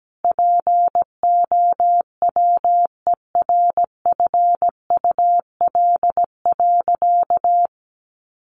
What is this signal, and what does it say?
Morse code, heard as a keyed tone, about 17 wpm, 700 Hz. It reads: POWERFUL.
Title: Sentences from Top 1300 Words 17wpm